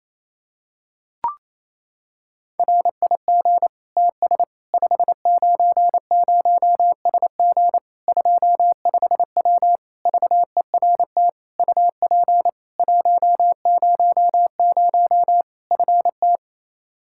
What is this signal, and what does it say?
Morse code, heard as a keyed tone, about 28 wpm, 700 Hz. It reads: RIG TS 590SG 25W VERT UP 100 FT
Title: Practice QSO - Exchange 2 - Info - 28wpm